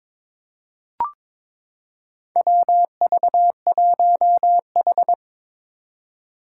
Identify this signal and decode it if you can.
Morse code, heard as a keyed tone, about 22 wpm, 700 Hz. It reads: WV1H
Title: Four Challenging Letter-Number Combinations 22wpm